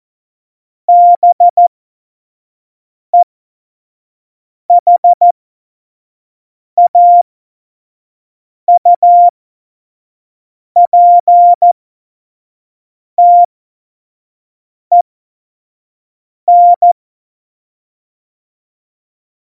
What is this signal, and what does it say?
Morse code, heard as a keyed tone, about 14 wpm, spaced as for 5 wpm, 700 Hz. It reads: BEHAUPTEN
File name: Morsekurs / Witze / Witz 27 March 2018, 25 F70